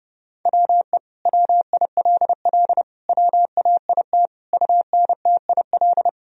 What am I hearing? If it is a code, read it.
Morse code, heard as a keyed tone, about 32 wpm, 700 Hz. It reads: WE WILL WAIT UNTIL
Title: Sentences from Top 1000 Words - Review Part 1 - 30wpm